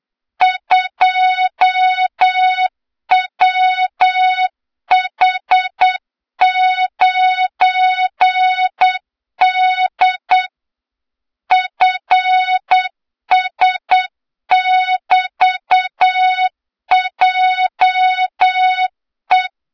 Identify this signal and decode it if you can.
Morse code, heard as a keyed tone, about 8 wpm, 3700 Hz, with noise. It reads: 2WH9D FS<BT>JE
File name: Morze-HA2MN / hang / 23 GYAKORLAT